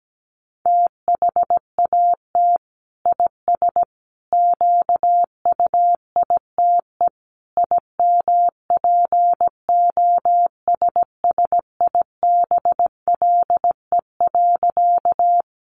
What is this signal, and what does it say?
Morse code, heard as a keyed tone, about 17 wpm, 700 Hz. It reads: THAT IS QUITE IMPOSSIBLE.
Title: Sentences from Top 1100 Words 17wpm